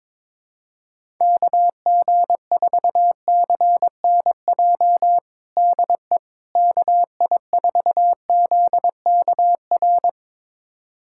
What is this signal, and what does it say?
Morse code, heard as a keyed tone, about 22 wpm, 700 Hz. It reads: KG4CNJ DE KI4ZKR
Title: USA Callsign DE Callsign - 6 Characters - 22wpm